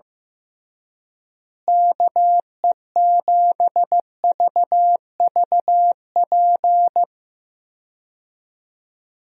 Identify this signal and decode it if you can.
Morse code, heard as a keyed tone, about 15 wpm, 700 Hz. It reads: KE7VVP
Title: USA Call Signs - 6 Characters - Repeated 3x - 15wpm